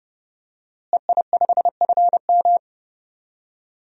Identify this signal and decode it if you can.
Morse code, heard as a keyed tone, about 32 wpm, 700 Hz. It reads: EI5FM
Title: DXCC-100 Call Signs - 5 Characters - Repeated 2x - 30wpm